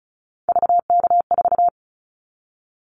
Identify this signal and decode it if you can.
Morse code, heard as a keyed tone, about 36 wpm, 700 Hz. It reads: VK4
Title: Callsign Prefixes for Top 50 DXCC - 35wpm